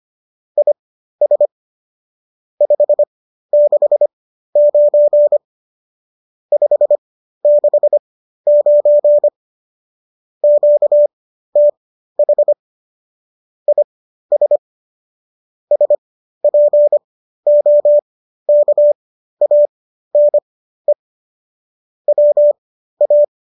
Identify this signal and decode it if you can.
Morse code, heard as a keyed tone, about 25 wpm, spaced as for 13 wpm, 590 Hz. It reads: IS 569 569 QTH IS SPOKANE WA
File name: qso206 13